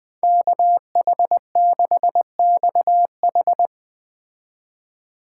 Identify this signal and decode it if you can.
Morse code, heard as a keyed tone, about 20 wpm, 700 Hz. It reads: KH6XH